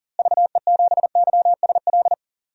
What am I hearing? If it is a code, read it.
Morse code, heard as a keyed tone, about 39 wpm, 700 Hz. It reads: VE7YSL
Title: DXCC-100 Call Signs - 6 Characters - Repeated 2x - 40wpm